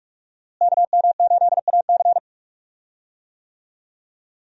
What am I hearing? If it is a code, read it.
Morse code, heard as a keyed tone, about 46 wpm, 700 Hz. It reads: KM8AC
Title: USA Call Signs - 5 Characters - Repeated 3x - 45wpm